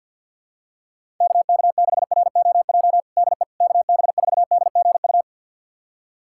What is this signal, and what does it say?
Morse code, heard as a keyed tone, about 49 wpm, 700 Hz. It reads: KK6ROJ DE KB4DGU
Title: USA Callsign DE Callsign - 6 Characters - 50wpm